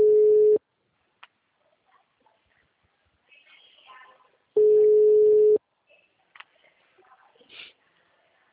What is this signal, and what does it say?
Ringback tone (Europe, 425 Hz).